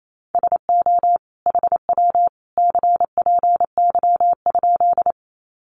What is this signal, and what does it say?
Morse code, heard as a keyed tone, about 28 wpm, 700 Hz. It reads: SO HW CPY?
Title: Common QSO Phrases 28wpm